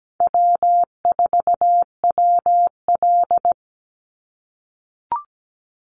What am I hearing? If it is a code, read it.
Morse code, heard as a keyed tone, about 17 wpm, 700 Hz. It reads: W4WL
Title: USA Call Signs 17wpm